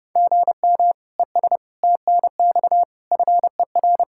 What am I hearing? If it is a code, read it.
Morse code, heard as a keyed tone, about 30 wpm, 700 Hz. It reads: GM ES TNX FER